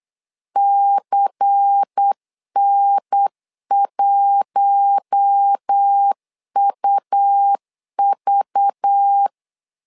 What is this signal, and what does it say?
Morse code, heard as a keyed tone, about 8 wpm, 800 Hz. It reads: CN1UV
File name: LEZIONI / 5 LBVC / 5 LBVC 8wpm 2935